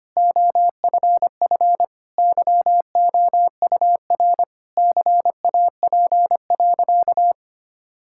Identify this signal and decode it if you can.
Morse code, heard as a keyed tone, about 25 wpm, 700 Hz. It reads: OFF YOUR CAP.
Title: Sentences from Top 1600 Words 25wpm